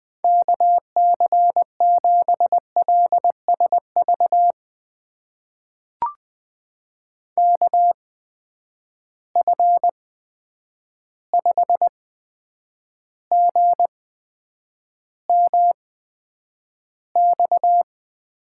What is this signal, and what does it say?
Morse code, heard as a keyed tone, about 20 wpm, spaced as for 9 wpm, 700 Hz. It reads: KC7LSV KF5GMX
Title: USA Call Signs - 6 Characters - 8x Character Spacing - 20wpm